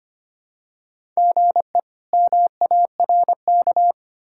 Morse code, heard as a keyed tone, about 25 wpm, 700 Hz. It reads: GE MARK
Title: K1USN SST QSO Practice - 25wpm